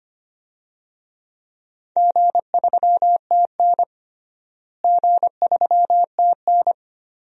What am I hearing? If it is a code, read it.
Morse code, heard as a keyed tone, about 25 wpm, 700 Hz. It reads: G3TN G3TN
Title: DXCC-100 Call Signs - 4 Characters - Repeated 3x - 25wpm